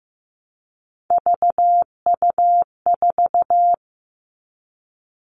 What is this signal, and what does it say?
Morse code, heard as a keyed tone, about 15 wpm, 700 Hz. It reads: VU4
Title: All Cut Numbers - 3-Digit 15wpm